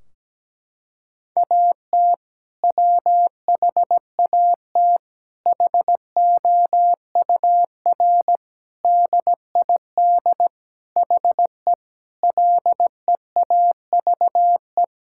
Morse code, heard as a keyed tone, about 17 wpm, 700 Hz. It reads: AT WHAT HOUR DID HE LEAVE
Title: Sentences from Top 1600 Words 17wpm